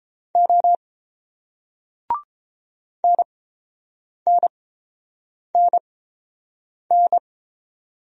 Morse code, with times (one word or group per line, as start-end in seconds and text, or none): 0.4-0.8 O
3.0-7.2 NNNN